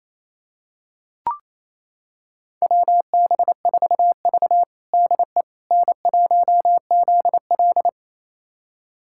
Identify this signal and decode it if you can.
Morse code, heard as a keyed tone, about 28 wpm, 690 Hz. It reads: WB4V DE N1ZL
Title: USA Callsign DE Callsign - 4 Characters - 28wpm